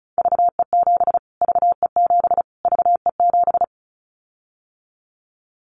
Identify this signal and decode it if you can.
Morse code, heard as a keyed tone, about 34 wpm, 700 Hz. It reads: VE7 VE7 VE7
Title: Callsign Prefixes for Top 25 DXCC - Repeated 3x - 35wpm